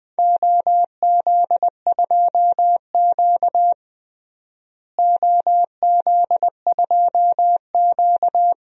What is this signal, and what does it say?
Morse code, heard as a keyed tone, about 21 wpm, 700 Hz. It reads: OZ2Q OZ2Q
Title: DXCC-100 Call Signs - 4 Characters - Repeated 3x - 20wpm